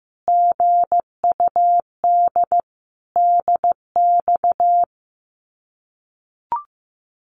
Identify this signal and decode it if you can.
Morse code, heard as a keyed tone, about 15 wpm, 700 Hz. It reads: GUD DX
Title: Common QSO Phrases 15wpm